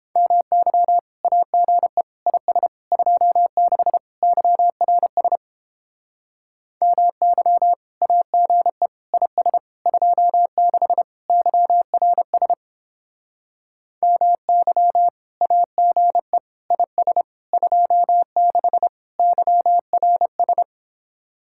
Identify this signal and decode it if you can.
Morse code, heard as a keyed tone, about 31 wpm, 700 Hz. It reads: MY AGE IS 26 YRS MY AGE IS 26 YRS MY AGE IS 26 YRS